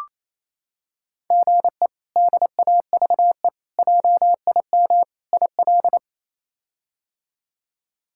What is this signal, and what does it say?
Morse code, heard as a keyed tone, about 28 wpm, 700 Hz. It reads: GE DAVE JIM IL
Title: K1USN SST QSO Practice - 28wpm